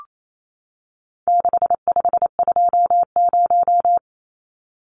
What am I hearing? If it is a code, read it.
Morse code, heard as a keyed tone, about 28 wpm, 700 Hz. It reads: 6520
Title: Numbers - 4-Digit 28wpm